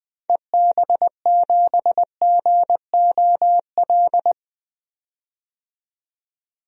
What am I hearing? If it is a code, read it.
Morse code, heard as a keyed tone, about 20 wpm, 700 Hz. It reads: EB7GOL
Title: DXCC-100 Call Signs - 6 Characters - 20wpm